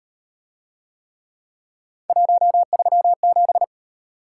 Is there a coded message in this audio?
Yes: Morse code from 2.1 to 3.6 s, heard as a keyed tone, about 38 wpm, 700 Hz. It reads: 137